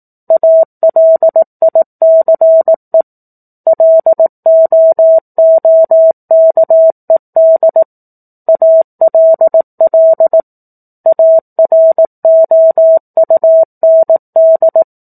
Morse code, heard as a keyed tone, about 18 wpm, 650 Hz. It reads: ALICE LOOKED ALL AROUND
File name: 04-Chapter-04 18 WPM